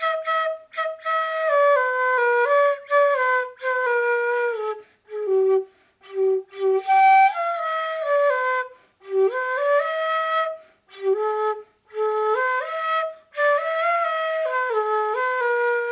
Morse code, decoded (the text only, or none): none